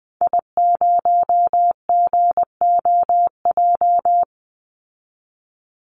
Morse code, heard as a keyed tone, about 20 wpm, 700 Hz. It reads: I0GOJ
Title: Call Signs Repeated 2x - 20wpm